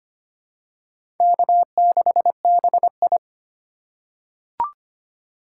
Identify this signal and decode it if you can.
Morse code, heard as a keyed tone, about 25 wpm, 700 Hz. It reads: K6BI
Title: USA Call Signs - 4 Characters - 2x Character Spacing - 25wpm